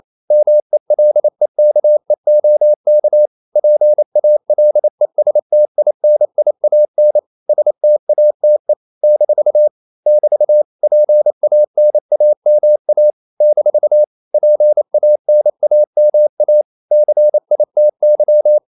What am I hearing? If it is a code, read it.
Morse code, heard as a keyed tone, about 28 wpm, 600 Hz. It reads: MELEKEOK PALESTINIAN STATE - <BT> PANAMA - PANAMA CITY
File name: CW / Practice / NationalCapitals28